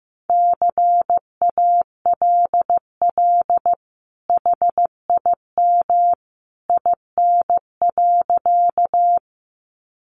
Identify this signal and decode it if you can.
Morse code, heard as a keyed tone, about 15 wpm, 700 Hz. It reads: CALL HIM IN.
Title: Sentences from Top 100 Words 15wpm